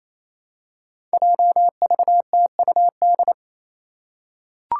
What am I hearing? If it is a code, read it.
Morse code, heard as a keyed tone, about 28 wpm, 700 Hz. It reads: JVTUD
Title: Five Challenging Letter-Number Combinations 28wpm